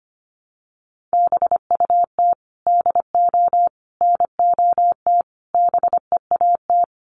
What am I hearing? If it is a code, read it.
Morse code, heard as a keyed tone, about 25 wpm, 700 Hz. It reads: BUT DO NOT BEAT
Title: Sentences from Top 1300 Words 25wpm